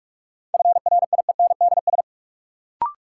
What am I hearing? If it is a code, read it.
Morse code, heard as a keyed tone, about 46 wpm, 700 Hz. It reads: FRIENDS
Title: Sentences from Top 1000 Words - Review Part 5 - New Word Drill - 45wpm